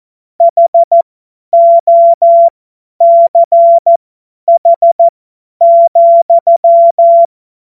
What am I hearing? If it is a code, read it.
Morse code, heard as a keyed tone, about 14 wpm, spaced as for 10 wpm, 680 Hz. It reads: HOCH,
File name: Morsekurs / Witze / Witz 28. Dezember 2021, 50 F70